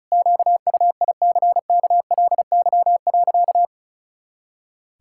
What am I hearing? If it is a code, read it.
Morse code, heard as a keyed tone, about 36 wpm, 700 Hz. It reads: QUICKLY.
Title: Pangram Challenge v2 - 35wpm